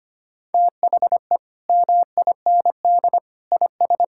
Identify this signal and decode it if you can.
Morse code, heard as a keyed tone, about 25 wpm, 700 Hz. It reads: THE MIND IS
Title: Sentences from Top 1000 Words - Review Part 6 - 25wpm